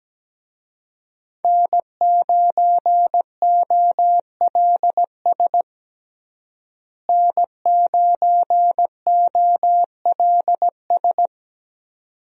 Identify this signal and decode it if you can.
Morse code, heard as a keyed tone, about 17 wpm, 700 Hz. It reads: N9OLS N9OLS
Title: USA Call Signs - 5 Characters - Repeated 2x - 17wpm